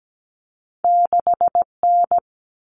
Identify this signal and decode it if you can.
Morse code, heard as a keyed tone, about 17 wpm, 700 Hz. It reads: 6N
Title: All Cut Numbers - 2-Digit 17wpm